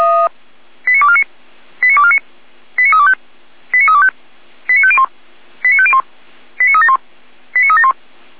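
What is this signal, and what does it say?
Five-tone selective call (ZVEI), heard as quick runs of five tones, tones of about 70 ms. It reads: 89249 89249 89325 89325 89501 89501 89371 89371
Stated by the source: SDR-received ZVEI five-tone selective call, code 89570.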